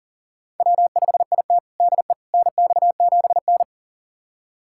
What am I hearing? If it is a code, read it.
Morse code, heard as a keyed tone, about 40 wpm, 700 Hz. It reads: W5IT DE NX7N
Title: USA Callsign DE Callsign - 4 Characters - 40wpm